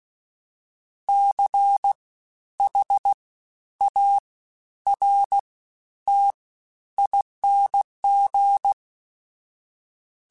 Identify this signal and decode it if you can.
Morse code, heard as a keyed tone, about 16 wpm, 800 Hz. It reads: C H A R T ING
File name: Special INGs-mp3s / Special INGs-mp3s / Fast ing @15